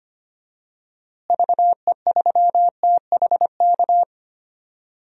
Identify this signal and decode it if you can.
Morse code, heard as a keyed tone, about 25 wpm, 700 Hz. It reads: VE3THK
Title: DXCC-100 Call Signs - 6 Characters - Repeated 3x - 25wpm